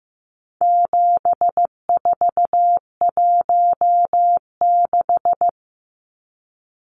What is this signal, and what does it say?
Morse code, heard as a keyed tone, about 15 wpm, 700 Hz. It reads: 7416
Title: Numbers - 4-Digit 15wpm